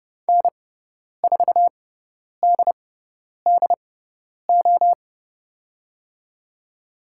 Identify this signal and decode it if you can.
Morse code, heard as a keyed tone, about 30 wpm, spaced as for 10 wpm, 700 Hz. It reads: N4DDO
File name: Lesson 16g - Letter c - Call Signs 30wpm